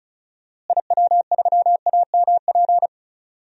Morse code, heard as a keyed tone, about 34 wpm, 700 Hz. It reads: IW3AMP
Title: DXCC-100 Call Signs - 6 Characters - Repeated 3x - 35wpm